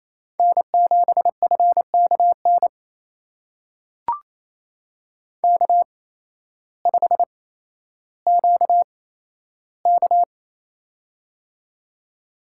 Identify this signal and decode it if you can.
Morse code, heard as a keyed tone, about 28 wpm, spaced as for 13 wpm, 700 Hz. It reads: N7FKN K5QK
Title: USA Call Signs - 8x Character Spacing - 28wpm